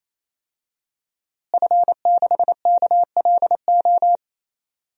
Morse code, heard as a keyed tone, about 28 wpm, 700 Hz. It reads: F6KLO
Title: DXCC-100 Call Signs - 5 Characters - Repeated 3x - 28wpm